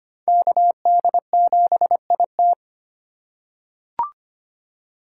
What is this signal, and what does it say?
Morse code, heard as a keyed tone, about 25 wpm, 700 Hz. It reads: KD7IT
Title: USA Call Signs - 5 Characters - 3x Character Spacing - 25wpm